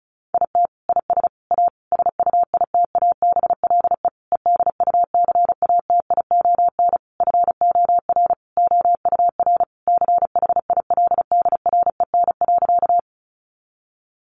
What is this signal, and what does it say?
Morse code, heard as a keyed tone, about 34 wpm, 700 Hz. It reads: IT IS A SUITABLE EDUCATION FOR OUR CHILDREN.
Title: Sentences from Top 1400 Words 35wpm